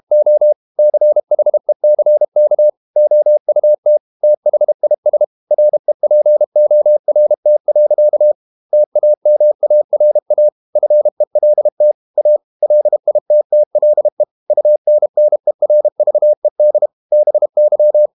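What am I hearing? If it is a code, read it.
Morse code, heard as a keyed tone, about 32 wpm, 600 Hz. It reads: O CHECK OUT THIS REPORT. TAMARA FELT A LITTLE UNNERVED BY
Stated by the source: Morse code practice at 32 wpm from CW med Gnister, reading Ken Follett's novel Never in English.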